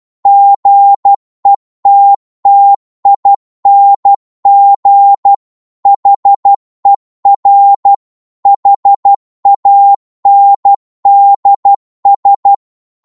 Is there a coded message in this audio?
Yes: Morse code from 0.3 to 12.5 s, heard as a keyed tone, about 12 wpm, 800 Hz. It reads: GETTING HER HANDS